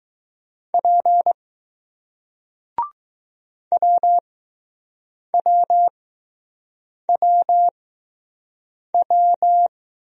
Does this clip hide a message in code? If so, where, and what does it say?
Yes: Morse code from 0.7 to 9.7 s, heard as a keyed tone, about 20 wpm, spaced as for 6 wpm, 700 Hz. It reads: P WWWW